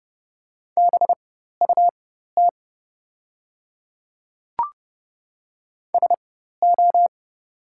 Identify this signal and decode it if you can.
Morse code, heard as a keyed tone, about 30 wpm, spaced as for 14 wpm, 700 Hz. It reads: BUT SO
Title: 4x Character Spacing - Top 100 Words - 30wpm